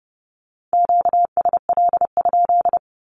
Morse code, heard as a keyed tone, about 30 wpm, 700 Hz. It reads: QSL?
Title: Q-Codes with Spoken Meaning - 30wpm